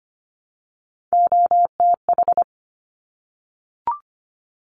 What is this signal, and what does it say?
Morse code, heard as a keyed tone, about 25 wpm, 700 Hz. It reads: OTH
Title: Top Three Letter Combinations in Words 25wpm